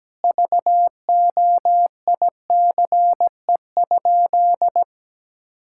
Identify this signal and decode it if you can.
Morse code, heard as a keyed tone, about 17 wpm, 700 Hz. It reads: VOICE?
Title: Sentences from Top 1000 Words - Review Part 6 - 17wpm